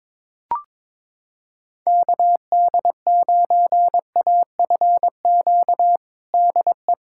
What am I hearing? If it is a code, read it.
Morse code, heard as a keyed tone, about 22 wpm, 700 Hz. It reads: KD9AFQ DE
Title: USA Callsign DE Callsign - 6 Characters - 22wpm